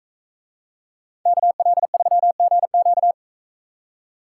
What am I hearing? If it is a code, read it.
Morse code, heard as a keyed tone, about 41 wpm, 700 Hz. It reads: KL3GQ